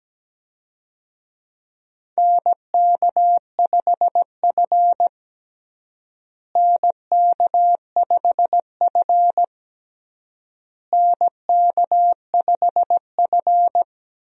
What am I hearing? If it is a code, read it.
Morse code, heard as a keyed tone, about 17 wpm, 700 Hz. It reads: NK5F NK5F NK5F
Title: USA Call Signs - 4 Characters - Repeated 3x - 17wpm